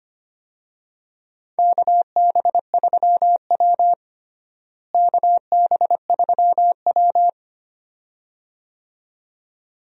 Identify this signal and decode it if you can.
Morse code, heard as a keyed tone, about 25 wpm, 700 Hz. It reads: KB3W KB3W
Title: USA Call Signs - 4 Characters - Repeated 2x - 25wpm